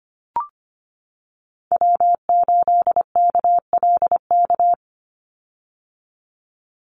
Morse code, heard as a keyed tone, about 25 wpm, 700 Hz. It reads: W8KLK
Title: USA Call Signs 25wpm